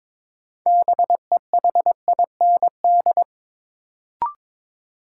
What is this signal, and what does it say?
Morse code, heard as a keyed tone, about 22 wpm, 700 Hz. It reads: BEHIND
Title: Sentences from Top 500 Words - New Word Drill - 22wpm